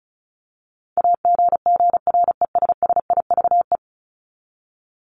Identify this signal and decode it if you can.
Morse code, heard as a keyed tone, about 34 wpm, 700 Hz. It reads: AGGRESSIVE
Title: Top 1501-1750 Words 35wpm